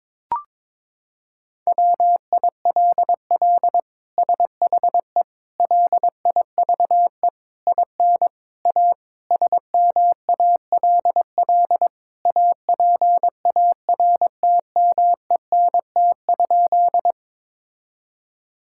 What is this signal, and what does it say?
Morse code, heard as a keyed tone, about 22 wpm, 700 Hz. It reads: WILL SHE LIVE IN A SMALL APARTMENT?
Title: Sentences from Top 1000 Words - Review Part 6 - 22wpm